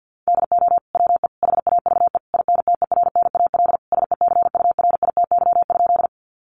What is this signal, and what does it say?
Morse code, heard as a keyed tone, about 50 wpm, 700 Hz. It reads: DO WE HAVE INTERNAL SECURITY?
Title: Sentences from Top 1400 Words 50wpm